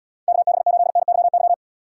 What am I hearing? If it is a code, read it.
Morse code, heard as a keyed tone, about 78 wpm, 700 Hz. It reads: KB8NYY